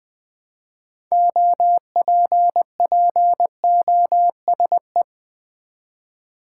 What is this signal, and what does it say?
Morse code, heard as a keyed tone, about 21 wpm, 700 Hz. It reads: OPPOSE
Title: Build A Word - 3 Letters At A Time - 6-Letter Words - 20wpm